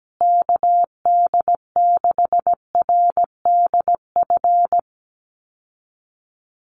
Morse code, heard as a keyed tone, about 17 wpm, 700 Hz. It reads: KD6RDF
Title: USA Call Signs Repeated 3x 17wpm